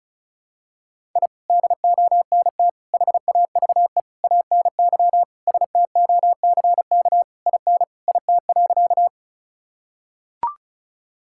Morse code, heard as a keyed tone, about 36 wpm, 700 Hz. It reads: I DONT HAVE ANY STOCK IN IT.
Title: Sentences from Top 1000 Words - Review Part 5 - 35wpm